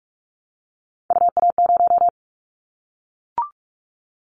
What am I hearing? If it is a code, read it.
Morse code, heard as a keyed tone, about 44 wpm, 700 Hz. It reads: UA0
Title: Callsign Prefixes for Top 25 DXCC - Repeated 3x - 45wpm